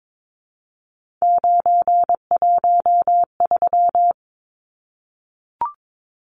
Morse code, heard as a keyed tone, about 22 wpm, 700 Hz. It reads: 913
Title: Numbers - 3-Digit 22wpm